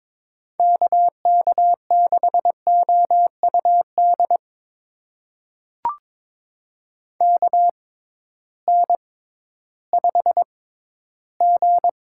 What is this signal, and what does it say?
Morse code, heard as a keyed tone, about 22 wpm, spaced as for 13 wpm, 700 Hz. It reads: KK6OUD KN5G
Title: USA Call Signs - 6x Character Spacing - 22wpm